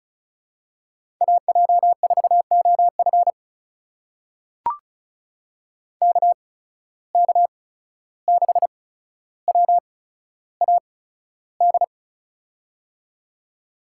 Morse code, heard as a keyed tone, about 36 wpm, spaced as for 14 wpm, 700 Hz. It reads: AJ4OF KK6WAD
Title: USA Call Signs - 8x Character Spacing - 35wpm